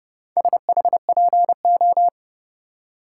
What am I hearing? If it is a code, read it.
Morse code, heard as a keyed tone, about 32 wpm, 700 Hz. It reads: SHPO